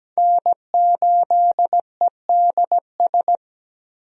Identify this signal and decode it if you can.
Morse code, heard as a keyed tone, about 17 wpm, 700 Hz. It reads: N8EDS
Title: USA Call Signs - 5 Characters - 17wpm